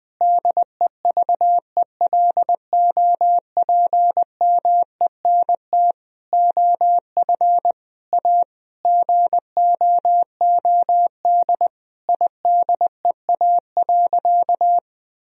Morse code, heard as a keyed tone, about 21 wpm, 700 Hz. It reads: DEVELOPMENT OF A GOOD IDEA.
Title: Sentences from Top 1000 Words - Review Part 4 - 20wpm